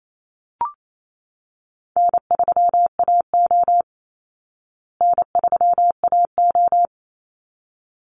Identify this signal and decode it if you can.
Morse code, heard as a keyed tone, about 28 wpm, 700 Hz. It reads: N3AO N3AO
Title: Easy Call Signs Repeated 2x - 28wpm